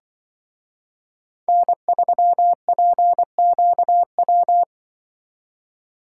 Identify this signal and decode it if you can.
Morse code, heard as a keyed tone, about 25 wpm, 700 Hz. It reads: N3PQW
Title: Speed Racing - USA 1x3 Call Signs 20wpm